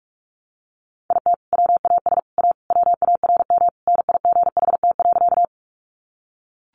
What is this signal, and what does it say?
Morse code, heard as a keyed tone, about 44 wpm, 700 Hz. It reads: IT WAS A WARM NIGHT.
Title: Sentences from Top 500 Words 45wpm